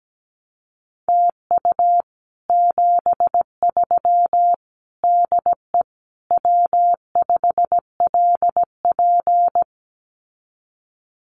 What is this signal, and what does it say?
Morse code, heard as a keyed tone, about 17 wpm, 700 Hz. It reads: TU 73 DE W5LP
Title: Common QSO Phrases Encore 17wpm